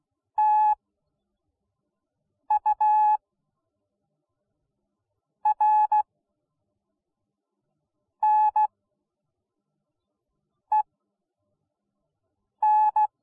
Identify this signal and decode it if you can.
Morse code, heard as a keyed tone, about 16 wpm, spaced as for 4 wpm, 850 Hz. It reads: TURNEN